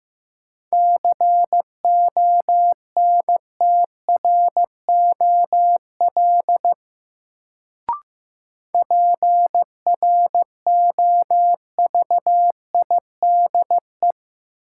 Morse code, heard as a keyed tone, about 15 wpm, 700 Hz. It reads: CONTROL PROVIDE
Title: Sentences from Top 1000 Words - Review Part 2 - New Word Drill - 15wpm